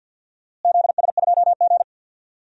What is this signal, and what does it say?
Morse code, heard as a keyed tone, about 51 wpm, 700 Hz. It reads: ZS2G
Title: DXCC-100 Call Signs - 4 Characters - Repeated 2x - 50wpm